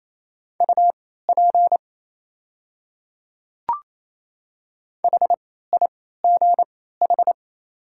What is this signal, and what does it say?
Morse code, heard as a keyed tone, about 27 wpm, spaced as for 16 wpm, 700 Hz. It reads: UP HIGH
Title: 3x Character Spacing - Top 200 Words - 28wpm